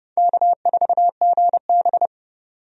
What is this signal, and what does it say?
Morse code, heard as a keyed tone, about 30 wpm, 700 Hz. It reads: K4GB
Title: USA Call Signs - 4 Characters - 30wpm